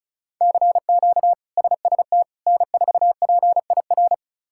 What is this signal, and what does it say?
Morse code, heard as a keyed tone, about 36 wpm, 700 Hz. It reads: CQ SST N4PIR
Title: K1USN SST QSO Practice - 35wpm